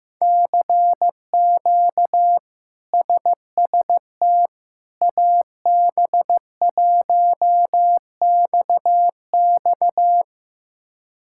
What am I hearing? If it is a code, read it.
Morse code, heard as a keyed tone, about 15 wpm, 700 Hz. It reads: CQ SST AB1XX
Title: K1USN SST QSO Practice - 15wpm